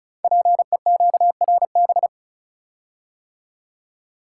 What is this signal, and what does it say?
Morse code, heard as a keyed tone, about 36 wpm, 690 Hz. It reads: PEQRB
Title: Code Groups - 5 Letters - 35wpm